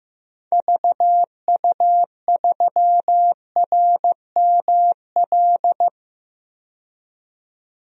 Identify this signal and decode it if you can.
Morse code, heard as a keyed tone, about 15 wpm, 700 Hz. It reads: VU3RML
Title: DXCC-100 Call Signs - 6 Characters - 15wpm